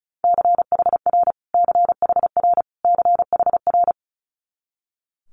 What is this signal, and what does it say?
Morse code, heard as a keyed tone, about 36 wpm, 700 Hz. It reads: CHR CHR CHR
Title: Multi-letter Phonemes Repeated 3x 35wpm